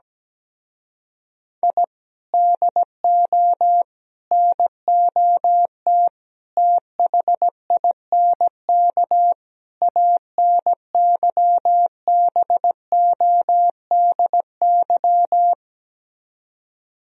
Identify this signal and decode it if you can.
Morse code, heard as a keyed tone, about 17 wpm, 700 Hz. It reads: I DO NOT THINK ANYBODY
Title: Speed-Racing - Sets of 5 Words 17wpm